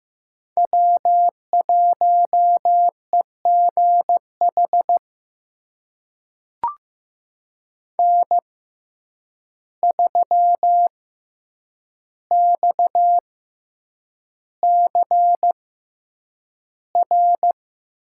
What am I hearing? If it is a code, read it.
Morse code, heard as a keyed tone, about 15 wpm, spaced as for 8 wpm, 700 Hz. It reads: W1EGH N3XCR
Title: USA Call Signs - 5 Characters - 6x Character Spacing - 15wpm